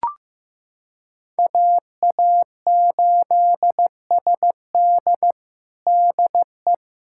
Morse code, heard as a keyed tone, about 15 wpm, 700 Hz. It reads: AA8SD DE
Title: USA Callsign DE Callsign - 5 Characters - 15wpm